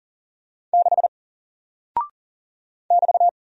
Morse code, heard as a keyed tone, about 40 wpm, 700 Hz. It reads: 6 <BT>
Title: Single Character (Letter:Number:Punctuation:Prosign) Alt 40wpm